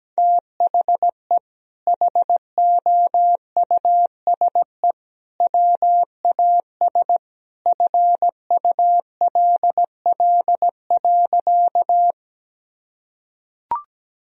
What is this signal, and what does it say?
Morse code, heard as a keyed tone, about 17 wpm, 700 Hz. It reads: THE HOUSE WAS FULL.
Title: Sentences from Top 1000 Words - Review Part 3 - 17wpm